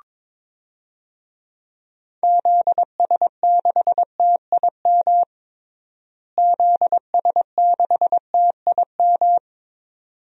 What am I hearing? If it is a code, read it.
Morse code, heard as a keyed tone, about 21 wpm, 700 Hz. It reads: ZS6TIM ZS6TIM
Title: DXCC-100 Call Signs - 6 Characters - Repeated 2x - 22wpm